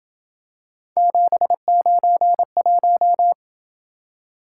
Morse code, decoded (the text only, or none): 791